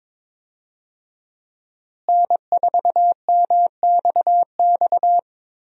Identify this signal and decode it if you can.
Morse code, heard as a keyed tone, about 22 wpm, 700 Hz. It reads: N4MXX
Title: USA Call Signs - 5 Characters - Repeated 3x - 22wpm